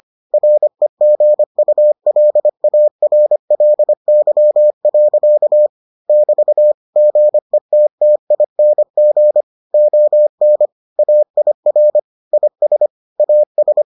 Morse code, heard as a keyed tone, about 25 wpm, 600 Hz. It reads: REGULARLY. <BT> GETTING ON AIR IS AS